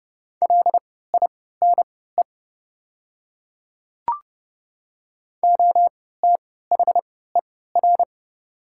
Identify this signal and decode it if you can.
Morse code, heard as a keyed tone, about 30 wpm, spaced as for 17 wpm, 700 Hz. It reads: LINE OTHER
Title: 3x Character Spacing - Top 300 Words - 30wpm